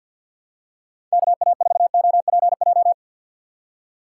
Morse code, heard as a keyed tone, about 50 wpm, 700 Hz. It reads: KA4OPJ
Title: USA Call Signs - 6 Characters - 8x Character Spacing - 50wpm